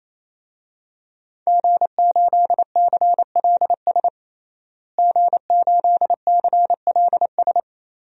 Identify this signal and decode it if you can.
Morse code, heard as a keyed tone, about 28 wpm, 700 Hz. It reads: G8CLS G8CLS
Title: DXCC-100 Call Signs - 5 Characters - Repeated 3x - 28wpm